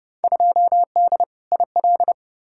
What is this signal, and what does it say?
Morse code, heard as a keyed tone, about 30 wpm, 700 Hz. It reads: 2D IL
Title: Field Day Exchanges - 30wpm